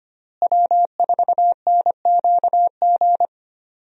Morse code, heard as a keyed tone, about 25 wpm, 700 Hz. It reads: W4NQG
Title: USA Call Signs - 5 Characters - Repeated 2x - 25wpm